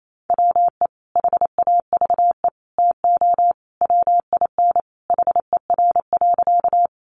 Morse code, heard as a keyed tone, about 28 wpm, 700 Hz. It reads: WE HAVE TO WIN HER.
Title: Sentences from Top 1500 Words 28wpm